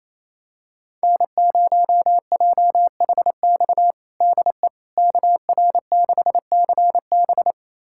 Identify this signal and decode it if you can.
Morse code, heard as a keyed tone, about 27 wpm, 700 Hz. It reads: N0JHX DE KR6CB
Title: USA Callsign DE Callsign - 5 Characters - 28wpm